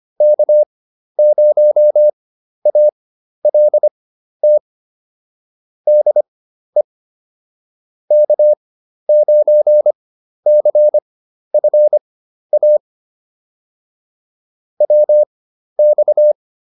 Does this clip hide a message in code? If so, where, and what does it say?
Yes: Morse code from 0.2 to 16.3 s, heard as a keyed tone, about 25 wpm, spaced as for 12 wpm, 600 Hz. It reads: K0ALT DE K9CFA WX